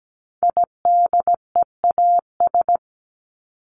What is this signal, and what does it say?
Morse code, heard as a keyed tone, about 17 wpm, 700 Hz. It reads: IDEAS
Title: ES:S Suffix - Top 100 Words with Base Words 17wpm